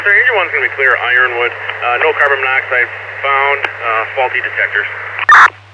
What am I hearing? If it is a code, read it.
Emergency Alert System (SAME) tones (header bursts), heard as harsh buzzing bursts, with speech. It is heard from 0.5 to 5.1 s.